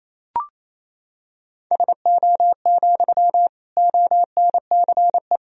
Morse code, heard as a keyed tone, about 28 wpm, 700 Hz. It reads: SO, ONCE
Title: Sentences from Top 300 Words and Abbreviations 28wpm